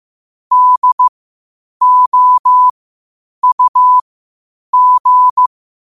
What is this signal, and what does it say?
Morse code, heard as a keyed tone, about 15 wpm, spaced as for 8 wpm, 1000 Hz. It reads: DOUG
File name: qotdfiles / 8wpm